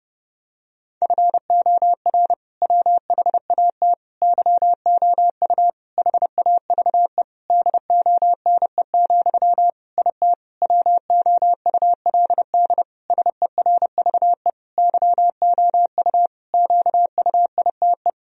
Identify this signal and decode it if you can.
Morse code, heard as a keyed tone, about 32 wpm, 700 Hz. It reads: FOR WHAT YOU HAVE DONE, IT WOULD SERVE YOU QUITE
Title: Sentences from Top 1000 Words - Review Part 2 - 30wpm